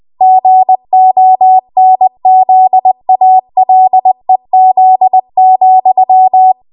Morse code, heard as a keyed tone, about 20 wpm, 750 Hz. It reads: GONZALEZ,
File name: Archive / 20 WPM / 20WPM